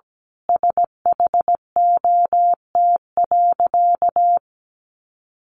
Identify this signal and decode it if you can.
Morse code, heard as a keyed tone, about 17 wpm, 700 Hz. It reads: SHOT.
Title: A Sound Of Thunder 17wpm